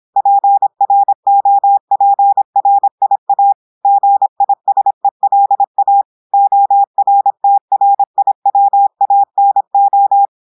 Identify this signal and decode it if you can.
Morse code, heard as a keyed tone, about 26 wpm, 800 Hz. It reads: PROPRIA GISELA ORTRIWANO
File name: audios / portugais / portugais texte en clair 1 vitesse 26 mots minutes